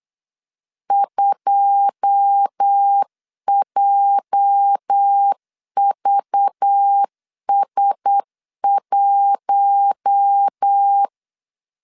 Morse code, heard as a keyed tone, about 8 wpm, 800 Hz. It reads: 2JVS1
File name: LEZIONI / 9 286 8wpm 3425